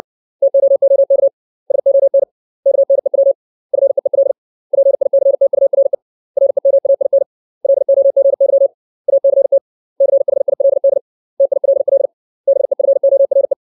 Morse code, heard as a keyed tone, about 60 wpm, 550 Hz. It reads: TOOK SON KNEW FEEL PICTURE REMAIN LOGY ACT CHILD TILL BEFORE
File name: 632 words / 632 Words 60WPM